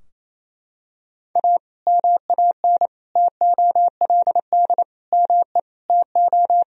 Morse code, heard as a keyed tone, about 28 wpm, 700 Hz. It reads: A MAN TOLD ME TO
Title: Sentences from Top 1000 Words - Review Part 3 - 28wpm